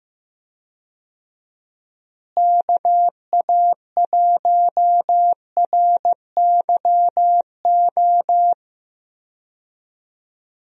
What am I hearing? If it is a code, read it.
Morse code, heard as a keyed tone, about 15 wpm, 700 Hz. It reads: KA1RYO
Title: USA Call Signs - 6 Characters - Repeated 3x - 15wpm